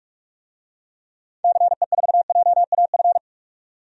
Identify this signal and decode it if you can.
Morse code, heard as a keyed tone, about 45 wpm, 700 Hz. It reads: CE4JAF
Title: DXCC-100 Call Signs - 6 Characters - 45wpm